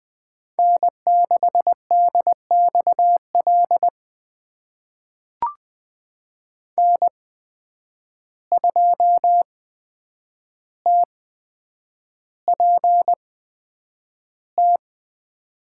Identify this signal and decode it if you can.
Morse code, heard as a keyed tone, about 21 wpm, spaced as for 9 wpm, 700 Hz. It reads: N6DXL N2TPT
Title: USA Call Signs - 5 Characters - 8x Character Spacing - 20wpm